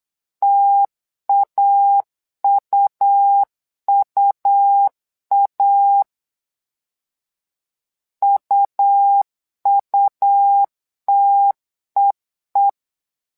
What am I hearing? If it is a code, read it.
Morse code, heard as a keyed tone, about 8 wpm, 800 Hz. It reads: TAUUA UUTEE
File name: LEZIONI / 2 UHDN / 2 U 8wpm 2028